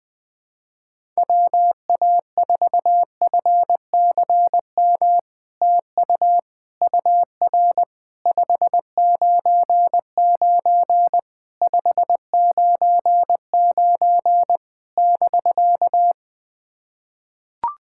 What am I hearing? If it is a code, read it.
Morse code, heard as a keyed tone, about 20 wpm, 700 Hz. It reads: WA4FCM TU UR 599 599 <BK>
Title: SOTA QSO Practice 20wpm